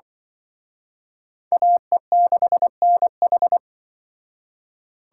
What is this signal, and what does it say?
Morse code, heard as a keyed tone, about 25 wpm, 700 Hz. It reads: AE6NH